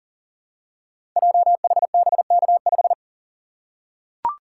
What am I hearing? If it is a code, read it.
Morse code, heard as a keyed tone, about 40 wpm, 700 Hz. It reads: JHBK5
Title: Code Groups - 5 Letter:Numbers - 40wpm